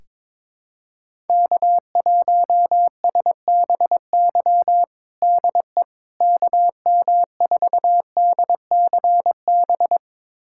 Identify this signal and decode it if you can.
Morse code, heard as a keyed tone, about 22 wpm, 700 Hz. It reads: K1SBY DE KM4DCB
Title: USA Callsign DE Callsign 22wpm